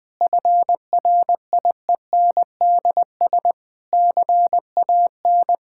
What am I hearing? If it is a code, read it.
Morse code, heard as a keyed tone, about 20 wpm, 700 Hz. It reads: FRIENDS CAN
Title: Difficult Sentences with Hints - 20wpm